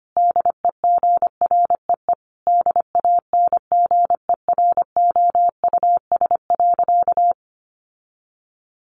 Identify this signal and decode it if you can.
Morse code, heard as a keyed tone, about 25 wpm, 700 Hz. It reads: DEGREE DANGEROUS.
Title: Sentences from Top 1500 Words 25wpm